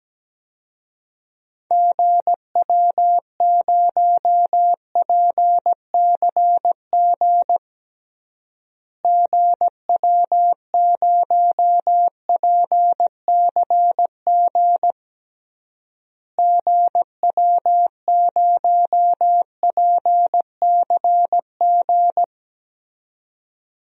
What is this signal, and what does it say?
Morse code, heard as a keyed tone, about 17 wpm, 700 Hz. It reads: GW0PCG GW0PCG GW0PCG
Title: DXCC-100 Call Signs - 6 Characters - Repeated 3x - 17wpm